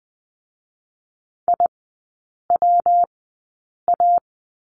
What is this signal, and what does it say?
Morse code, heard as a keyed tone, about 20 wpm, 700 Hz. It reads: I W A
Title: 114 Lesson 19e - Letter w - Three Characters 20wpm